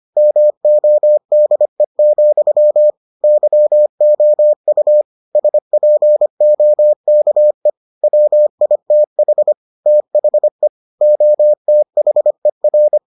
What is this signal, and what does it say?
Morse code, heard as a keyed tone, about 25 wpm, 600 Hz. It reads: MODE, YOU SPOKE WITH THE OTHER